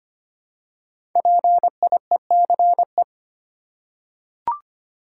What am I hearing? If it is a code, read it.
Morse code, heard as a keyed tone, about 25 wpm, 700 Hz. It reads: PIECE
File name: 5 Letter Words - First 3 Letters Given - 25wpm